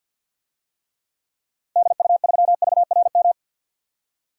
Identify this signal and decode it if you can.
Morse code, heard as a keyed tone, about 53 wpm, 700 Hz. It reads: DU3VRM
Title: DXCC-100 Call Signs - 6 Characters - 50wpm